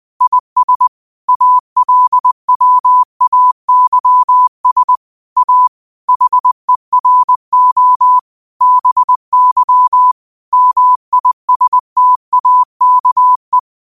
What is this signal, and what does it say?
Morse code, heard as a keyed tone, about 20 wpm, 1000 Hz. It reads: IS ALWAYS A HERO BY MISTAKE
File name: qotdfiles / 20wpm